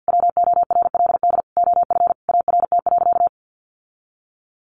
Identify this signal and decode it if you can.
Morse code, heard as a keyed tone, about 50 wpm, 700 Hz. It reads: WORLD OF ART.
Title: Sentences from Top 1500 Words 50wpm